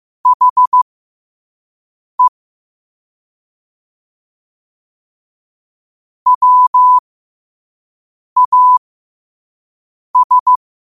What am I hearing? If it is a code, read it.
Morse code, heard as a keyed tone, about 15 wpm, spaced as for 5 wpm, 1000 Hz. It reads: HE WAS
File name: qotdfiles / 5wpm